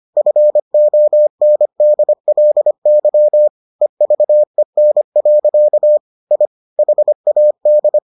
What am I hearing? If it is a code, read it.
Morse code, heard as a keyed tone, about 25 wpm, 600 Hz. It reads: FONDLY EVEN. I HAD